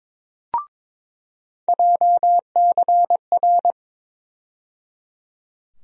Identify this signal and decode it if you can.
Morse code, heard as a keyed tone, about 22 wpm, 700 Hz. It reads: JCR
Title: USA Call Sign Postfixes - 3 Letters 22wpm